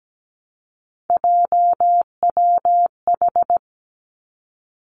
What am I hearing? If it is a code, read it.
Morse code, heard as a keyed tone, about 17 wpm, 700 Hz. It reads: JWH
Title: Three Challenging Letter-Number Combinations 17wpm